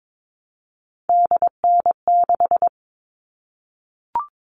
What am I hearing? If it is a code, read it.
Morse code, heard as a keyed tone, about 22 wpm, 700 Hz. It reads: DN6
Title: All Cut Numbers - 3-Digit 22wpm